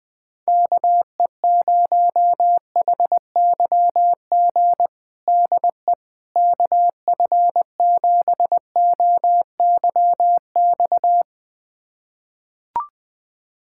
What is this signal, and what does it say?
Morse code, heard as a keyed tone, about 20 wpm, 700 Hz. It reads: KE0HYG DE KF7OYX
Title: USA Callsign DE Callsign - 6 Characters - 20wpm